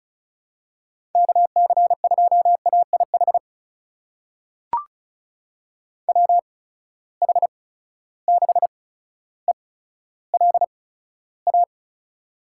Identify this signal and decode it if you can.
Morse code, heard as a keyed tone, about 34 wpm, spaced as for 15 wpm, 700 Hz. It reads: KC2AIH WH6ELA
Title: USA Call Signs - 6 Characters - 8x Character Spacing - 35wpm